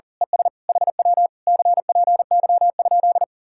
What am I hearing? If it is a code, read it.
Morse code, heard as a keyed tone, about 38 wpm, 700 Hz. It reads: ES HW CPY?
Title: Speed-Racing Common QSO Phrases - 40wpm